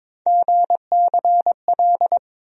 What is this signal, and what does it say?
Morse code, heard as a keyed tone, about 22 wpm, 700 Hz. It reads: GCL
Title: USA Call Sign Postfixes - 3 Letters 22wpm